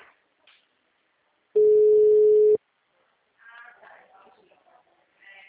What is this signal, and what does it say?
Ringback tone (Europe, 425 Hz), with speech.